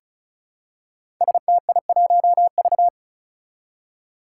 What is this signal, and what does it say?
Morse code, heard as a keyed tone, about 36 wpm, 700 Hz. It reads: STI1V
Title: Five Challenging Letter-Number Combinations 35wpm